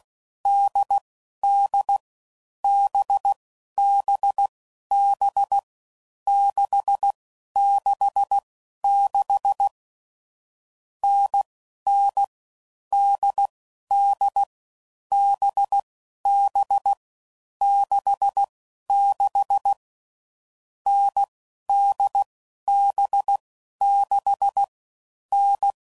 Morse code, heard as a keyed tone, about 16 wpm, 800 Hz. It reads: D D B B B 6 6 6 N N D D B B 6 6 N D B 6 N
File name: Combinations / Difficult Combos / NDB NDB6